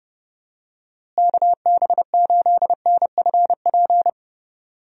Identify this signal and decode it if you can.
Morse code, heard as a keyed tone, about 31 wpm, 700 Hz. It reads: KB8NFP
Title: USA Call Signs - 6 Characters - Repeated 2x - 30wpm